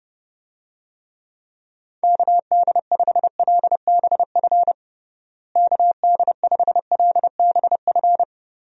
Morse code, heard as a keyed tone, about 30 wpm, 700 Hz. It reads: KD5LBF KD5LBF
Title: USA Call Signs - 6 Characters - Repeated 3x - 30wpm